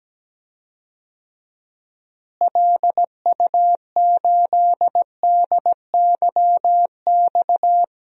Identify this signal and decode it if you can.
Morse code, heard as a keyed tone, about 17 wpm, 700 Hz. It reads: LU8DYX
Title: DXCC-100 Call Signs - 6 Characters - Repeated 3x - 17wpm